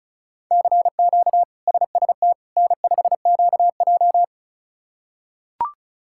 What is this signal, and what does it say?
Morse code, heard as a keyed tone, about 36 wpm, 700 Hz. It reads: CQ SST N5QJ
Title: K1USN SST QSO Practice - 35wpm